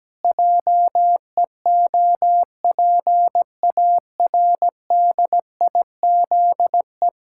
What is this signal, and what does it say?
Morse code, heard as a keyed tone, about 17 wpm, 700 Hz. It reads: JEOPARDIZE
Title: Pangram Challenge v2 - 17wpm